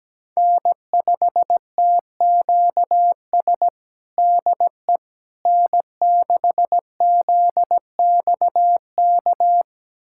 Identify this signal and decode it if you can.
Morse code, heard as a keyed tone, about 17 wpm, 700 Hz. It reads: N5TQS DE N6ZXK
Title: USA Callsign DE Callsign - 5 Characters - 17wpm